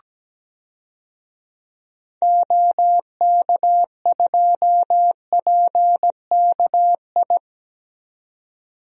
Morse code, heard as a keyed tone, about 17 wpm, 700 Hz. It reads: OK2PKI
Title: DXCC-100 Call Signs - 6 Characters - 17wpm